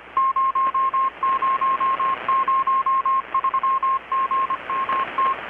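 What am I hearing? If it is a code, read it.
Morse code, heard as a keyed tone, about 25 wpm, 1070 Hz. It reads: 0003GMT
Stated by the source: Morse from KPH's Night of Nights XX maritime radiotelegraphy event, received in Chiapas, Mexico about 3,800 km away on a Kenwood R-600 with a grounded 10 m long-wire antenna.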